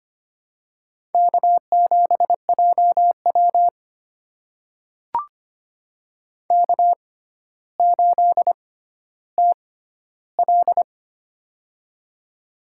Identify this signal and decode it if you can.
Morse code, heard as a keyed tone, about 25 wpm, spaced as for 13 wpm, 700 Hz. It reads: K7JW K8TL
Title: USA Call Signs - 4 Characters - 6x Character Spacing - 25wpm